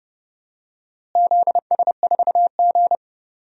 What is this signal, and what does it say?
Morse code, heard as a keyed tone, about 30 wpm, 700 Hz. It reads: ZS4G
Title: DXCC-100 Call Signs - 4 Characters - Repeated 2x - 30wpm